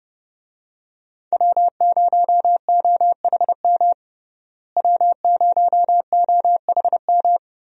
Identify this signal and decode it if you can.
Morse code, heard as a keyed tone, about 30 wpm, 700 Hz. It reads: W0OHM W0OHM
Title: USA Call Signs - 5 Characters - Repeated 2x - 30wpm